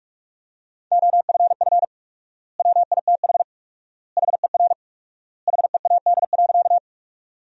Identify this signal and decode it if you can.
Morse code, heard as a keyed tone, about 46 wpm, 700 Hz. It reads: OFF WITH HER HEAD.
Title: 4x Word-Spacing - Sentences from Top 300 Words - 45wpm